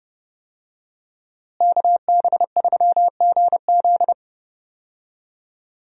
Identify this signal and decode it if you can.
Morse code, heard as a keyed tone, about 31 wpm, 700 Hz. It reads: KB3GZ
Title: USA Call Signs - 5 Characters - Repeated 2x - 30wpm